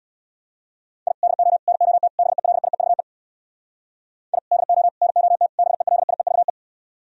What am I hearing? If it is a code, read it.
Morse code, heard as a keyed tone, about 64 wpm, 700 Hz. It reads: I DO NOT BELIEVE I DO NOT BELIEVE